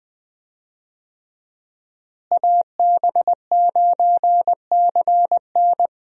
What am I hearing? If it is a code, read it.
Morse code, heard as a keyed tone, about 20 wpm, 700 Hz. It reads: AB9CN
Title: USA Call Signs - 5 Characters - Repeated 3x - 20wpm